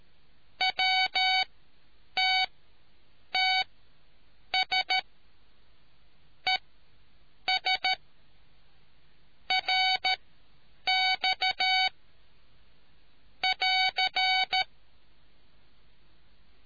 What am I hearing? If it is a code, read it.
Morse code, heard as a keyed tone, about 13 wpm, spaced as for 7 wpm, 750 Hz. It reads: WTTS ES RX <AR>